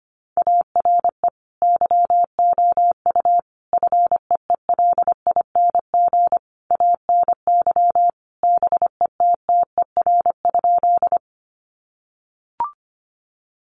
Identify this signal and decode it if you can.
Morse code, heard as a keyed tone, about 25 wpm, 700 Hz. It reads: ARE YOU FEELING ANY BETTER?
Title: Sentences from Top 1200 Words 25wpm